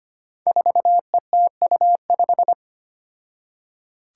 Morse code, heard as a keyed tone, about 25 wpm, 700 Hz. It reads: 4ETU5
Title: Five Challenging Letter-Number Combinations 25wpm